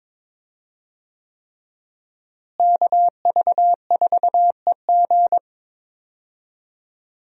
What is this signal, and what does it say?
Morse code, heard as a keyed tone, about 22 wpm, 700 Hz. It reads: KV4EG
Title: USA Call Signs - 5 Characters - Repeated 3x - 22wpm